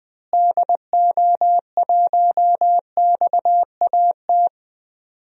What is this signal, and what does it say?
Morse code, heard as a keyed tone, about 20 wpm, 700 Hz. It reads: DO1XAT
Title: DXCC-100 Call Signs - 6 Characters - Repeated 2x - 20wpm